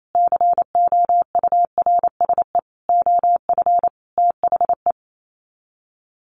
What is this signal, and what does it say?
Morse code, heard as a keyed tone, about 28 wpm, 700 Hz. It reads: COURSE OF THE
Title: Sets of 5 Words Encore 28wpm